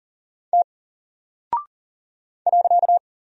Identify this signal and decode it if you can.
Morse code, heard as a keyed tone, about 40 wpm, 700 Hz. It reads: T .
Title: Single Character (Letter:Number:Punctuation:Prosign) Alt 40wpm